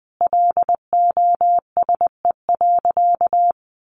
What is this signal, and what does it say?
Morse code, heard as a keyed tone, about 20 wpm, 700 Hz. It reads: LOSE.
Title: Sentences from Top 1100 Words 20wpm